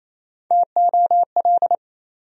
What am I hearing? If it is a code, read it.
Morse code, heard as a keyed tone, about 28 wpm, 700 Hz. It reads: TOL
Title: USA Call Sign Postfixes - 3 Letters 28wpm